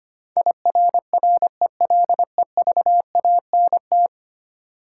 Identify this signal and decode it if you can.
Morse code, heard as a keyed tone, about 25 wpm, 700 Hz. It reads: IRRELEVANT
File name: 10 Letter Words - First 3 Letters Given - 25wpm